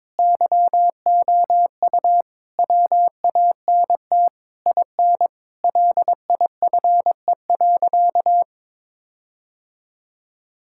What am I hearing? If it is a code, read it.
Morse code, heard as a keyed tone, about 22 wpm, 700 Hz. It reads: YOU WANT IN LIFE.
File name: Sentences with 7-Word Phrases - 22wpm